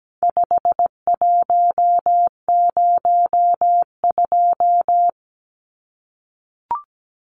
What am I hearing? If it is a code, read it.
Morse code, heard as a keyed tone, about 17 wpm, 700 Hz. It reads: 5102
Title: Numbers - 4-Digit 17wpm